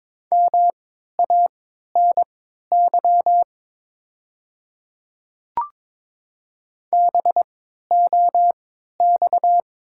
Morse code, heard as a keyed tone, about 22 wpm, spaced as for 13 wpm, 700 Hz. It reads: MANY BOX
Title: 3x Character Spacing - Top 500 Words - 22wpm